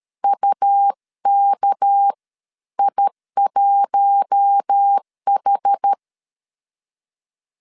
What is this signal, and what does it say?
Morse code, heard as a keyed tone, about 13 wpm, spaced as for 11 wpm, 800 Hz. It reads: UKI1H
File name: CW / DET / 1 9 1500C S A 12wpm 3800